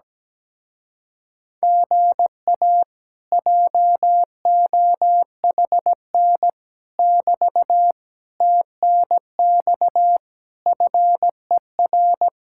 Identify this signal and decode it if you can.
Morse code, heard as a keyed tone, about 16 wpm, 700 Hz. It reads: GA JOHN <BT> TNX FER
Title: Speed-Racing Common QSO Phrases - 17wpm